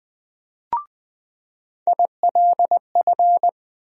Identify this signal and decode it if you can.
Morse code, heard as a keyed tone, about 21 wpm, 700 Hz. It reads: ILF
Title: USA Call Sign Postfixes - 3 Letters 20wpm